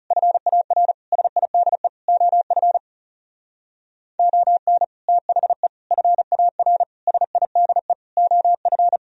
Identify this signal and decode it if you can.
Morse code, heard as a keyed tone, about 36 wpm, 700 Hz. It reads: FAR SIDE OF ON THE FAR SIDE OF
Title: Speed-Racing - Sets of 5 Words 30wpm